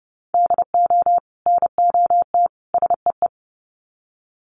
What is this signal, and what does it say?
Morse code, heard as a keyed tone, about 30 wpm, 700 Hz. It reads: DO NOT SEE
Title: Sets of 3 Words 30wpm